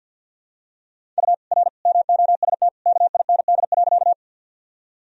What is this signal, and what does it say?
Morse code, heard as a keyed tone, about 50 wpm, 700 Hz. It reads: U R MOST KIND.
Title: Sentences from Top 200 Words and Abbreviations 50wpm